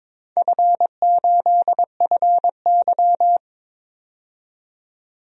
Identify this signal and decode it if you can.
Morse code, heard as a keyed tone, about 22 wpm, 690 Hz. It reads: F8FY
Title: DXCC-100 Call Signs - 4 Characters - 22wpm